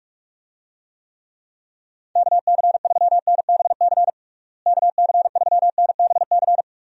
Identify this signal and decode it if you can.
Morse code, heard as a keyed tone, about 42 wpm, 700 Hz. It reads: KC3NBC KC3NBC
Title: USA Call Signs - 6 Characters - Repeated 2x - 45wpm